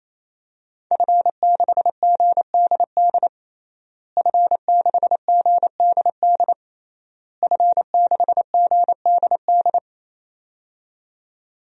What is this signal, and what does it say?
Morse code, heard as a keyed tone, about 28 wpm, 700 Hz. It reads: F6GDD F6GDD F6GDD
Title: DXCC-100 Call Signs - 5 Characters - Repeated 3x - 28wpm